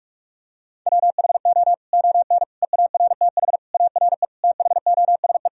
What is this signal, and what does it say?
Morse code, heard as a keyed tone, about 41 wpm, 700 Hz. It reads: WHO ON EARTH ARE THOSE
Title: Sentences from Top 1000 Words - Review Part 1 - 45wpm